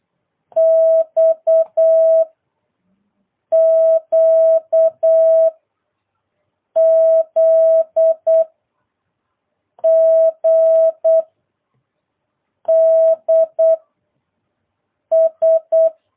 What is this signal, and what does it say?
Morse code, heard as a keyed tone, about 8 wpm, 650 Hz. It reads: X Q Z G D S